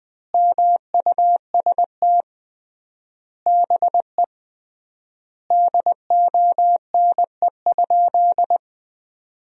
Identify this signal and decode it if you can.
Morse code, heard as a keyed tone, about 20 wpm, 700 Hz. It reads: MUST BE DONE?
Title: 3x Word-Spacing - Sentences from Top 500 Words - 20wpm